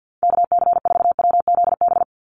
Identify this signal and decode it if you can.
Morse code, heard as a keyed tone, about 48 wpm, 700 Hz. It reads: KC4WZB
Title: NA SOTA Activator Callsigns 50wpm